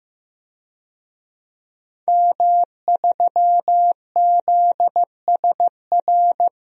Morse code, heard as a keyed tone, about 15 wpm, 700 Hz. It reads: M3ZSR
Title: DXCC-100 Call Signs - 5 Characters - 15wpm